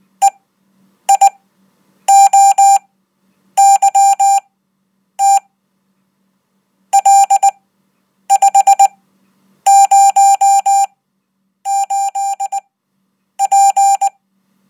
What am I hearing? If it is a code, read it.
Morse code, heard as a keyed tone, about 19 wpm, spaced as for 8 wpm, 790 Hz. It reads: EIOYT L508P
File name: Code